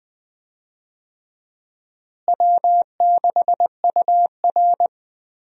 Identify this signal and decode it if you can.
Morse code, heard as a keyed tone, about 20 wpm, 700 Hz. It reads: W6UR
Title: USA Call Signs - 4 Characters - Repeated 3x - 20wpm